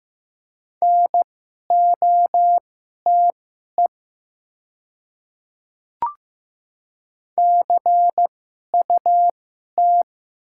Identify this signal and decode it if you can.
Morse code, heard as a keyed tone, about 15 wpm, spaced as for 11 wpm, 700 Hz. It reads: NOTE CUT
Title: 2x Character Spacing - Top 500 Words - 15wpm